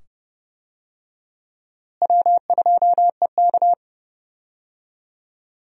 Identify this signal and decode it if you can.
Morse code, heard as a keyed tone, about 30 wpm, 700 Hz. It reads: W2EK
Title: USA Call Signs - 4 Characters - Repeated 2x - 30wpm